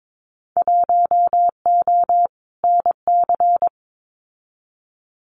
Morse code, heard as a keyed tone, about 22 wpm, 700 Hz. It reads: 1O NC
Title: Winter Field Day Exchanges 22wpm